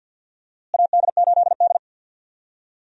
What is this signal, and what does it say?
Morse code, heard as a keyed tone, about 46 wpm, 700 Hz. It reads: AD8D
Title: USA Call Signs - 4 Characters - Repeated 2x - 50wpm